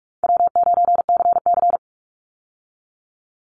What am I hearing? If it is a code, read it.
Morse code, heard as a keyed tone, about 44 wpm, 700 Hz. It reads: W9CC
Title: Easy Call Signs Repeated 2x - 45wpm